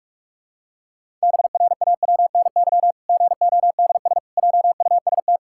Morse code, heard as a keyed tone, about 46 wpm, 700 Hz. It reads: BRAWNY GODS JUST
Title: Pangram Challenge v2 - 45wpm